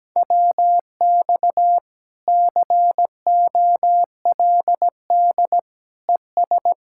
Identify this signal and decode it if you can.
Morse code, heard as a keyed tone, about 17 wpm, 700 Hz. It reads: WX COLD ES
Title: Speed-Racing Common QSO Phrases - 17wpm